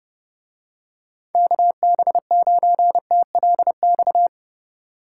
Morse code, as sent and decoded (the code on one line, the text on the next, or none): -.- -... ----. - .-.. -..-
KB9TLX